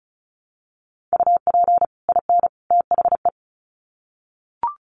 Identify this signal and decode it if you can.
Morse code, heard as a keyed tone, about 34 wpm, 700 Hz. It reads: UP IN THE
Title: Sets of 3 Words 35wpm